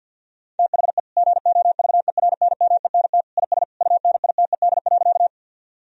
Morse code, heard as a keyed tone, about 50 wpm, 700 Hz. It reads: THE GOVERNMENT IS UNITED.
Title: Sentences from Top 1000 Words - Review Part 4 - 50wpm